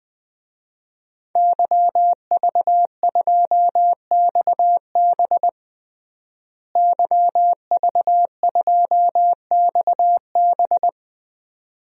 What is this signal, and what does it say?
Morse code, heard as a keyed tone, about 20 wpm, 700 Hz. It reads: YV2XB YV2XB
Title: DXCC-100 Call Signs - 5 Characters - Repeated 2x - 20wpm